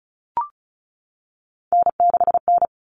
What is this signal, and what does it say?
Morse code, heard as a keyed tone, about 34 wpm, 700 Hz. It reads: N6N
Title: All Cut Numbers - 3-Digit 35wpm